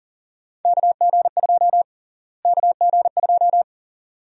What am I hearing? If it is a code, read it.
Morse code, heard as a keyed tone, about 37 wpm, 700 Hz. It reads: KG2 KG2
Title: USA Call Sign Prefixes - Repeated 2x - 40wpm